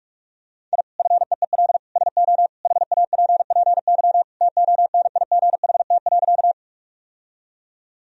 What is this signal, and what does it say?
Morse code, heard as a keyed tone, about 43 wpm, 700 Hz. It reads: I FEEL SO HAPPY TONIGHT.
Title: Sentences from Top 1000 Words - Review Part 3 - 45wpm